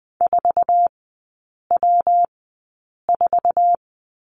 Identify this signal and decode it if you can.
Morse code, heard as a keyed tone, about 20 wpm, spaced as for 8 wpm, 700 Hz. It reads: 4W4
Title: 191 Lesson 31e - Letter v - Three Characters 20wpm